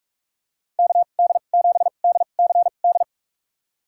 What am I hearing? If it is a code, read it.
Morse code, heard as a keyed tone, about 46 wpm, spaced as for 32 wpm, 700 Hz. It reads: KD7DCD
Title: USA Call Signs - 6 Characters - 2x Character Spacing - 45wpm